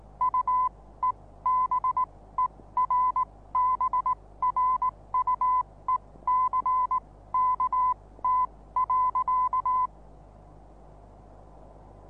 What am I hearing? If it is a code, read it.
Morse code, heard as a keyed tone, about 19 wpm, spaced as for 15 wpm, 1010 Hz. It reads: UEBERBRUECKT.
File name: repos / Morsekurs / DJ4UF Morsen26